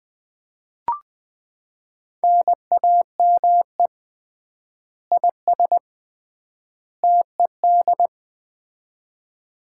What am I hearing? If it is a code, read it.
Morse code, heard as a keyed tone, about 20 wpm, 700 Hz. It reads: NAME IS TED
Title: Common QSO Phrases - 3x Word-Spacing - 20wpm